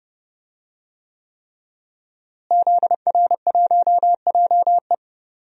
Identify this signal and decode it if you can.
Morse code, heard as a keyed tone, about 32 wpm, 700 Hz. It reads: ZR1JE
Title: DXCC-100 Call Signs - 5 Characters - Repeated 3x - 30wpm